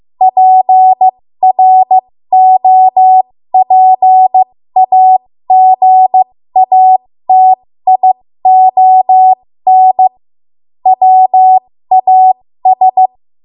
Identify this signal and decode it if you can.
Morse code, heard as a keyed tone, about 15 wpm, spaced as for 13 wpm, 750 Hz. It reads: PROPAGATION WAS
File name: Morse / 13WPM